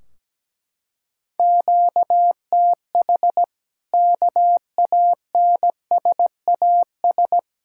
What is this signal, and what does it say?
Morse code, heard as a keyed tone, about 17 wpm, 700 Hz. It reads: QTH KANSAS
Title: Practice QSO - Exchange 1 - Beginning - 17wpm